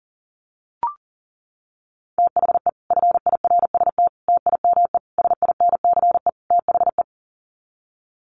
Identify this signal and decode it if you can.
Morse code, heard as a keyed tone, about 40 wpm, 700 Hz. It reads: THE FIRST TIME SINCE THE
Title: Sets of 5 Words Encore 40wpm